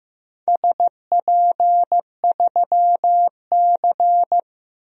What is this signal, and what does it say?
Morse code, heard as a keyed tone, about 15 wpm, 700 Hz. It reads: SP3C
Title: DXCC-100 Call Signs - 4 Characters - 15wpm